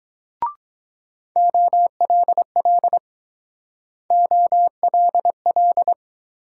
Morse code, heard as a keyed tone, about 23 wpm, 700 Hz. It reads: OLL OLL